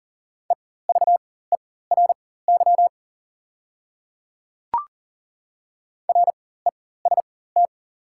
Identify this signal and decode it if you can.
Morse code, heard as a keyed tone, about 37 wpm, spaced as for 18 wpm, 690 Hz. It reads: EVERY REST
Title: 4x Character Spacing - Top 500 Words - 40wpm